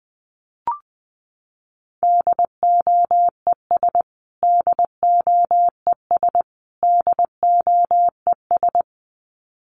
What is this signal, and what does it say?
Morse code, heard as a keyed tone, about 20 wpm, 700 Hz. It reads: DOES DOES DOES
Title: Top 200 Words Repeated 20wpm